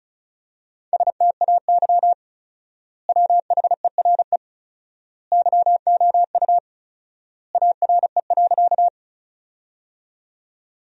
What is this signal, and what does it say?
Morse code, heard as a keyed tone, about 34 wpm, 700 Hz. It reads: STAY WHERE YOU ARE.
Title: 4x Word-Spacing - Sentences from Top 500 Words - 35wpm